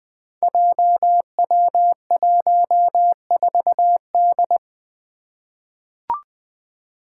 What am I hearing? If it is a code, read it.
Morse code, heard as a keyed tone, about 20 wpm, 700 Hz. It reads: JW14D
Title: Five Challenging Letter-Number Combinations 20wpm